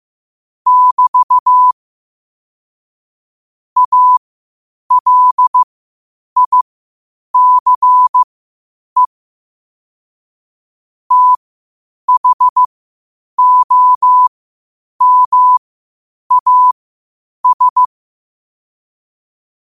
Morse code, heard as a keyed tone, about 15 wpm, spaced as for 8 wpm, 1000 Hz. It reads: <BT> ALICE THOMAS
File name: qotdfiles / 8wpm